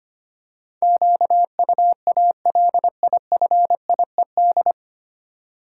Morse code, heard as a keyed tone, about 25 wpm, 690 Hz. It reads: QUALIFIED
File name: practice / 25wpm / 9 Letter Words - First 3 Letters Given - 25wpm